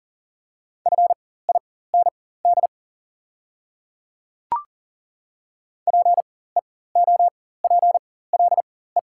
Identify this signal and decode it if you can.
Morse code, heard as a keyed tone, about 40 wpm, spaced as for 19 wpm, 700 Hz. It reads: FIND PEOPLE
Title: 4x Character Spacing - Top 100 Words - 40wpm